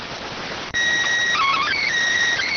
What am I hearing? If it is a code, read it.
SSTV (slow-scan television), PD 290.